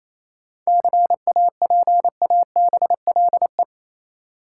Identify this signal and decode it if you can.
Morse code, heard as a keyed tone, about 28 wpm, 700 Hz. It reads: CAPABLE
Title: Sentences from Top 1200 Words - New Word Drill Alt - 28wpm